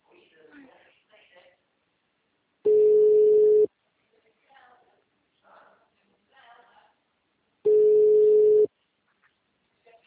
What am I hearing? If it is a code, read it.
Ringback tone (Europe, 425 Hz), with speech.